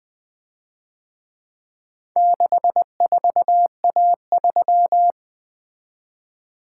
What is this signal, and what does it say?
Morse code, heard as a keyed tone, about 24 wpm, 700 Hz. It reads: 64A3
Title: Speed-Racing - Four Letter-Number Combinations - 20wpm